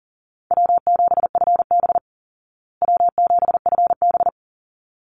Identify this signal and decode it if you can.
Morse code, heard as a keyed tone, about 40 wpm, 700 Hz. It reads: W7FB W7FB
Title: Easy Call Signs Repeated 2x - 40wpm